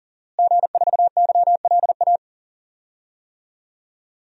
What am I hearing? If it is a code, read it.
Morse code, heard as a keyed tone, about 40 wpm, 700 Hz. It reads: G4YLA
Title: DXCC-100 Call Signs - 5 Characters - Repeated 3x - 40wpm